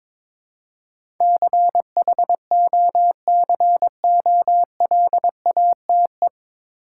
Morse code, heard as a keyed tone, about 21 wpm, 700 Hz. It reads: CHOCOLATE
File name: 9 Letter Words - First 3 Letters Given - 22wpm